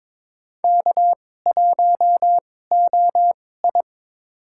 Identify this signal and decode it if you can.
Morse code, heard as a keyed tone, about 22 wpm, spaced as for 16 wpm, 700 Hz. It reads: K1OI
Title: USA Call Signs - 2x Character Spacing - 22wpm